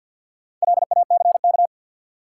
Morse code, heard as a keyed tone, about 52 wpm, 700 Hz. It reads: LACK
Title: Sets of 3 Words - New Word Drill - 50wpm